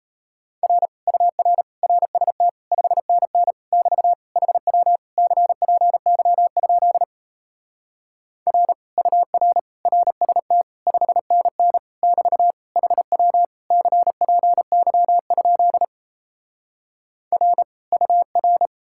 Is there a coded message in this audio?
Yes: Morse code from 0.6 to 18.7 s, heard as a keyed tone, about 38 wpm, 700 Hz. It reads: R UR RST 5NN <BT> HW CPY? R UR RST 5NN <BT> HW CPY? R UR